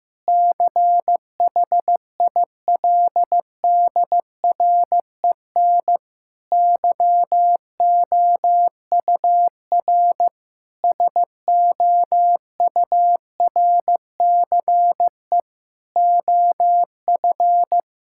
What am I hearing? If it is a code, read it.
Morse code, heard as a keyed tone, about 14 wpm, 700 Hz. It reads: CHILDREN YOUR SOURCE OF
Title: Sentences from Top 1000 Words - Review Part 5 - 15wpm